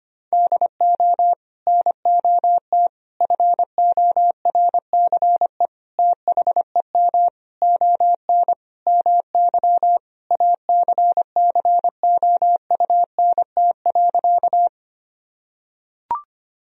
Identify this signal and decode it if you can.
Morse code, heard as a keyed tone, about 25 wpm, 700 Hz. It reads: DO NOT FORCE THEM ON MY ACCOUNT.
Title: Sentences from Top 1000 Words - Review Part 3 - 25wpm